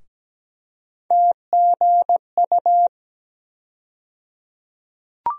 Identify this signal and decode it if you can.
Morse code, heard as a keyed tone, about 17 wpm, 700 Hz. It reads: TGU
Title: USA Call Sign Postfixes - 3 Letters 17wpm